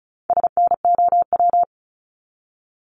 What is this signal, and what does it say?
Morse code, heard as a keyed tone, about 34 wpm, 700 Hz. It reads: SNOW
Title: Top 500 Words 35wpm